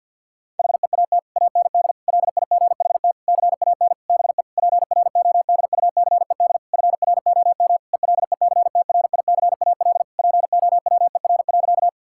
Morse code, heard as a keyed tone, about 52 wpm, 700 Hz. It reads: HEAT AND LIGHT CAN BE PRODUCED FROM ELECTRICAL POWER.
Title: Sentences from Top 1000 Words - Review Part 6 - 50wpm